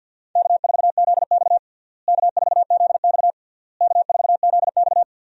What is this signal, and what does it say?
Morse code, heard as a keyed tone, about 46 wpm, 700 Hz. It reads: K4ZX K4ZX K4ZX
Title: USA Call Signs - 4 Characters - Repeated 3x - 50wpm